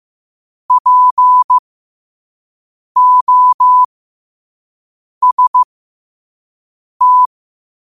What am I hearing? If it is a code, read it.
Morse code, heard as a keyed tone, about 15 wpm, spaced as for 5 wpm, 1000 Hz. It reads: POST
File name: qotdfiles / 5wpm